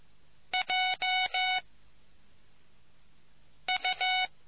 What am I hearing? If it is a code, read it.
Morse code, heard as a keyed tone, about 15 wpm, 740 Hz. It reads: J U